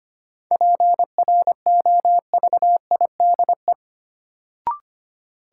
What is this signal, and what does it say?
Morse code, heard as a keyed tone, about 25 wpm, 700 Hz. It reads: PROVIDE
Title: Sentences from Top 1000 Words - Review Part 2 - New Word Drill - 25wpm